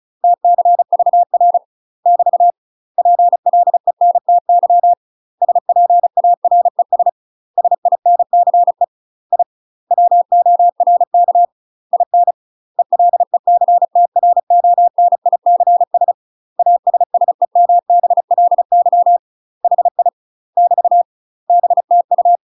Morse code, heard as a keyed tone, about 34 wpm, 700 Hz. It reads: TCVR <BT> PLENTY SPARES SINCE I WORK IN ELECTRONICS ASSEMBLY HI <BT> BTU
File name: UK MASTER QSOs TRAINING/35WPM